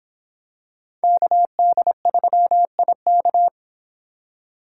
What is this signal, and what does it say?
Morse code, heard as a keyed tone, about 26 wpm, 700 Hz. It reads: KD3IK